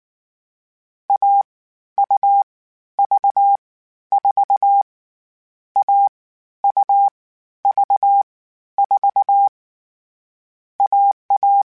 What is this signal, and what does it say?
Morse code, heard as a keyed tone, about 19 wpm, spaced as for 12 wpm, 800 Hz. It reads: AUV4 AUV4 AA
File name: CW Code Groups-mp3s / Sets mp3s / AUV AUV4 sp @18